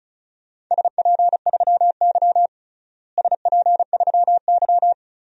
Morse code, heard as a keyed tone, about 36 wpm, 700 Hz. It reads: SP3Y SP3Y
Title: DXCC-100 Call Signs - 4 Characters - Repeated 3x - 35wpm